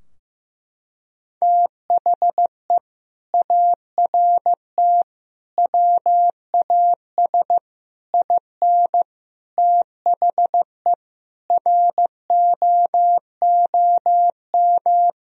Morse code, heard as a keyed tone, about 15 wpm, 700 Hz. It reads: THE ART WAS IN THE ROOM
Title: Sentences from Top 1000 Words - Review Part 3 - 15wpm